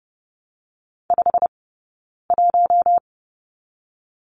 Morse code, heard as a keyed tone, about 30 wpm, 700 Hz. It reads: 5 1
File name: course / 30:10wpm / 086 Lesson 15d - Number 5 - Two Characters 30wpm